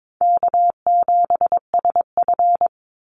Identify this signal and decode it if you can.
Morse code, heard as a keyed tone, about 22 wpm, 700 Hz. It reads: K7SF
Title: Easy Call Signs - 22wpm